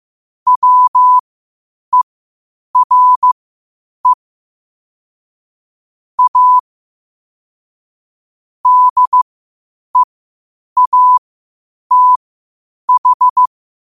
Morse code, heard as a keyed tone, about 15 wpm, spaced as for 8 wpm, 1000 Hz. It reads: WERE A DEATH
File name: qotdfiles / 8wpm